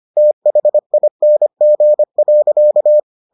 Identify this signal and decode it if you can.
Morse code, heard as a keyed tone, about 25 wpm, 600 Hz. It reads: THING.